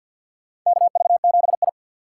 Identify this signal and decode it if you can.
Morse code, heard as a keyed tone, about 52 wpm, 700 Hz. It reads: KV7I
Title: USA Call Signs - 4 Characters - 3x Character Spacing - 50wpm